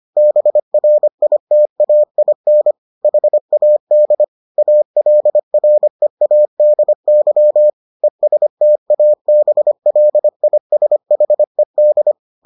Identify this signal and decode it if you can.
Morse code, heard as a keyed tone, about 25 wpm, 600 Hz. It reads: BRITAIN HAD ALREADY ESTABLISHED